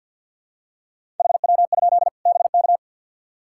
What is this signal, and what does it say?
Morse code, heard as a keyed tone, about 47 wpm, 700 Hz. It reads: HW? BK
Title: Practice QSO - Exchange 2 - Info - 50wpm